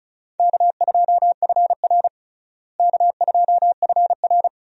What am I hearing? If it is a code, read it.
Morse code, heard as a keyed tone, about 36 wpm, 700 Hz. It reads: K2FR K2FR
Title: USA Call Signs - 4 Characters - Repeated 2x - 35wpm